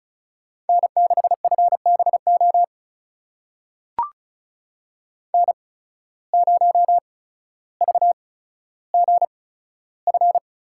Morse code, heard as a keyed tone, about 36 wpm, spaced as for 15 wpm, 700 Hz. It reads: N6FBO N0VGF
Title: USA Call Signs - 5 Characters - 8x Character Spacing - 35wpm